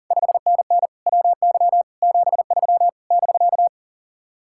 Morse code, heard as a keyed tone, about 51 wpm, 700 Hz. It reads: 5NN WY 73 <BK>